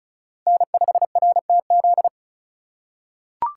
Morse code, heard as a keyed tone, about 36 wpm, 700 Hz. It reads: N5RTZ
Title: USA Call Signs - 5 Characters - 2x Character Spacing - 35wpm